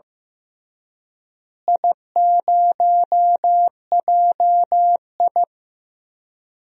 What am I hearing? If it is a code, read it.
Morse code, heard as a keyed tone, about 15 wpm, 700 Hz. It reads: I0JI
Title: DXCC-100 Call Signs - 4 Characters - Repeated 3x - 15wpm